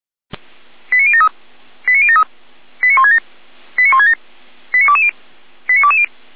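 Five-tone selective call (ZVEI), heard as quick runs of five tones, tones of about 70 ms. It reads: 89073 89073 89167 89167 89220 89220
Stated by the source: Five-tone ZVEI selective call received by SDR.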